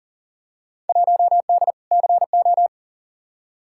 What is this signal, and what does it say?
Morse code, heard as a keyed tone, about 38 wpm, 700 Hz. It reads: 1D CO
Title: Field Day Exchanges - 40wpm